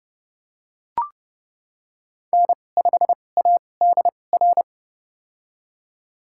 Morse code, heard as a keyed tone, about 30 wpm, spaced as for 22 wpm, 700 Hz. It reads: N5ADR
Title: USA Call Signs - 2x Character Spacing - 30wpm